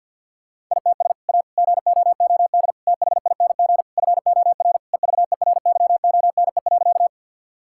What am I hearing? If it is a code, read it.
Morse code, heard as a keyed tone, about 51 wpm, 700 Hz. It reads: ITS A GOOD THING FOR EVERYONE.
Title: Sentences from Top 1000 Words - Review Part 5 - 50wpm